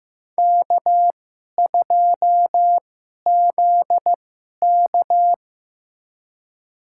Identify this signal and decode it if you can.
Morse code, heard as a keyed tone, about 15 wpm, spaced as for 11 wpm, 700 Hz. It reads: K2ZK
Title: USA Call Signs - 4 Characters - 2x Character Spacing - 15wpm